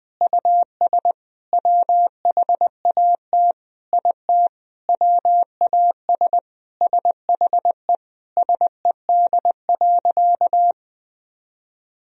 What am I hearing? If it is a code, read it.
Morse code, heard as a keyed tone, about 20 wpm, 700 Hz. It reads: US WHAT IT WAS SHE SED.
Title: Sentences from Top 400 Words and Abbreviations 20wpm